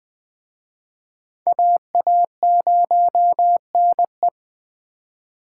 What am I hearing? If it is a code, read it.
Morse code, heard as a keyed tone, about 21 wpm, 700 Hz. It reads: AA0NE
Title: Speed Racing - USA 2x2 Call Signs 17wpm